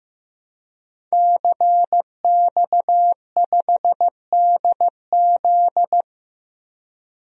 Morse code, heard as a keyed tone, about 15 wpm, 700 Hz. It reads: CX5DZ
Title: DXCC-100 Call Signs - 5 Characters - Repeated 3x - 15wpm